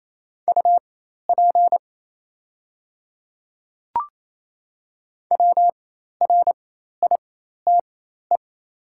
Morse code, heard as a keyed tone, about 27 wpm, spaced as for 13 wpm, 700 Hz. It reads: UP WRITE
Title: 4x Character Spacing - Top 100 Words - 28wpm